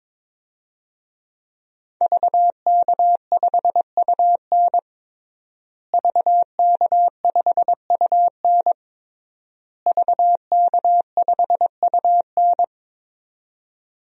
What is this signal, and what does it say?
Morse code, heard as a keyed tone, about 22 wpm, 700 Hz. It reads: VK5UN VK5UN VK5UN
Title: DXCC-100 Call Signs - 5 Characters - Repeated 3x - 22wpm